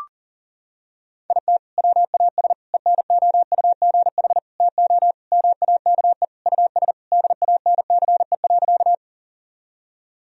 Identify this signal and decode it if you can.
Morse code, heard as a keyed tone, about 40 wpm, 700 Hz. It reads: IT WAS ENOUGH TO MAKE US DANCE.
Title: Sentences from Top 1000 Words - Review Part 4 - 40wpm